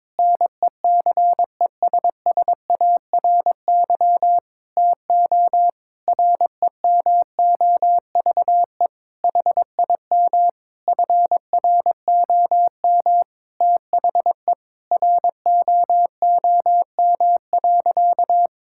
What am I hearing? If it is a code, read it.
Morse code, heard as a keyed tone, about 22 wpm, 700 Hz. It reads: NECESSARY TO REMOVE HIM FROM THE ROOM.
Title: Sentences from Top 1000 Words - Review Part 5 - 22wpm